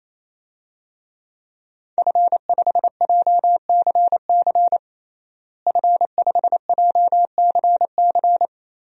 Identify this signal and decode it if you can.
Morse code, heard as a keyed tone, about 27 wpm, 700 Hz. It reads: F5JCC F5JCC
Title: DXCC-100 Call Signs - 5 Characters - Repeated 3x - 28wpm